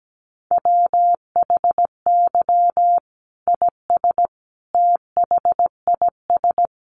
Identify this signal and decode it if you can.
Morse code, heard as a keyed tone, about 17 wpm, 700 Hz. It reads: WHY IS THIS
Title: Sentences from Top 1200 Words 17wpm